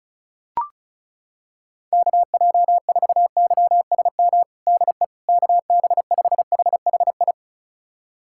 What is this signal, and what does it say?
Morse code, heard as a keyed tone, about 36 wpm, 700 Hz. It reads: KJ4YSM DE KB5HHI
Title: USA Callsign DE Callsign - 6 Characters - 35wpm